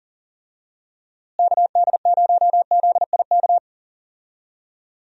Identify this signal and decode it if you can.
Morse code, heard as a keyed tone, about 40 wpm, 700 Hz. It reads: KD0ZIK
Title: USA Call Signs - 6 Characters - Repeated 3x - 40wpm